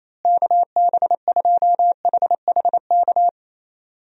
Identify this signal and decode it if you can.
Morse code, heard as a keyed tone, about 27 wpm, 700 Hz. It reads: KB2HHK
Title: USA Call Signs - 6 Characters - Repeated 2x - 28wpm